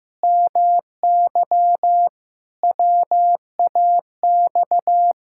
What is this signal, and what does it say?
Morse code, heard as a keyed tone, about 16 wpm, 700 Hz. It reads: MY WAX
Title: Pangram Challenge v2 - 15wpm